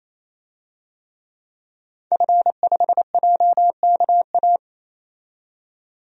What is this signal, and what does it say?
Morse code, heard as a keyed tone, about 28 wpm, 700 Hz. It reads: F5JKA
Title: DXCC-100 Call Signs - 5 Characters - 28wpm